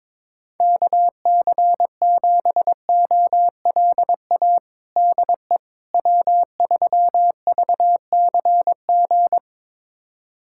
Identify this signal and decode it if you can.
Morse code, heard as a keyed tone, about 21 wpm, 700 Hz. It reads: KC7OLA DE W3VCG
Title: USA Callsign DE Callsign 22wpm